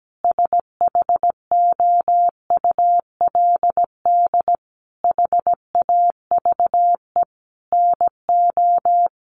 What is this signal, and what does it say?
Morse code, heard as a keyed tone, about 17 wpm, 700 Hz. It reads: SHOULD HAVE NO
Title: Sentences from Top 1500 Words 17wpm